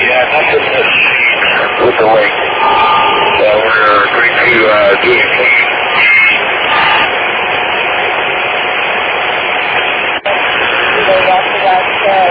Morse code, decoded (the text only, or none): none